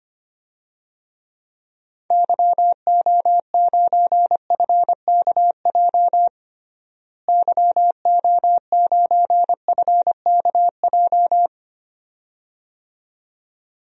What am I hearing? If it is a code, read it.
Morse code, heard as a keyed tone, about 25 wpm, 700 Hz. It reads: YO9FKJ YO9FKJ
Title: DXCC-100 Call Signs - 6 Characters - Repeated 2x - 25wpm